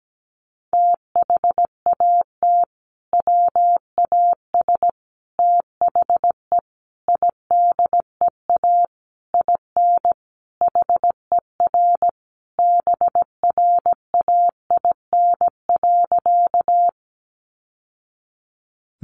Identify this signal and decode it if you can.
Morse code, heard as a keyed tone, about 17 wpm, 700 Hz. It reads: THAT WAS THE IDEA IN HER BRAIN.
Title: Sentences from Top 1100 Words 17wpm